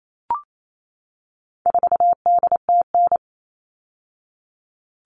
Morse code, heard as a keyed tone, about 28 wpm, 700 Hz. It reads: 4DTN
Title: All Cut Numbers - 4-Digit 28wpm